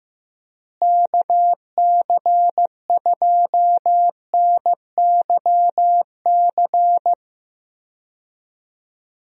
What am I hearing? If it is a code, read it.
Morse code, heard as a keyed tone, about 15 wpm, 700 Hz. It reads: KC2NYC
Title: USA Call Signs - 6 Characters - Repeated 3x - 15wpm